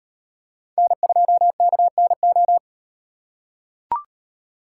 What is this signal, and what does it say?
Morse code, heard as a keyed tone, about 35 wpm, 700 Hz. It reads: N2KNO